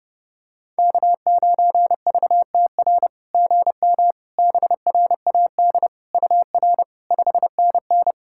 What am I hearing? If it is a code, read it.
Morse code, heard as a keyed tone, about 30 wpm, 690 Hz. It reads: K9VTR GM BRAD UR 5NN
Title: SOTA QSO Practice 30wpm